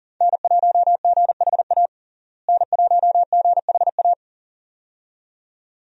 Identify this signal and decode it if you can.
Morse code, heard as a keyed tone, about 40 wpm, 700 Hz. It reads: N1GHA N1GHA
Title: USA Call Signs - 5 Characters - Repeated 2x - 40wpm